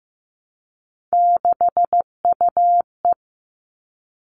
Morse code, heard as a keyed tone, about 15 wpm, 700 Hz. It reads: 6UE
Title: All Cut Numbers - 3-Digit 15wpm